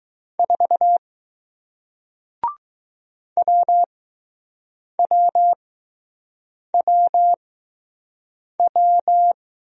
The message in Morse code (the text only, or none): 4 WWWW